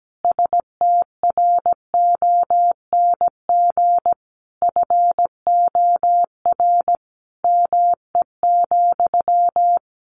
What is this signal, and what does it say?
Morse code, heard as a keyed tone, about 17 wpm, 700 Hz. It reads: STRONG FOR ME,
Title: Sentences from Top 600 Words 17wpm